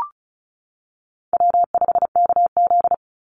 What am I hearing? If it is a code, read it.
Morse code, heard as a keyed tone, about 34 wpm, 700 Hz. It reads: W5KZ
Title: USA Call Signs 35wpm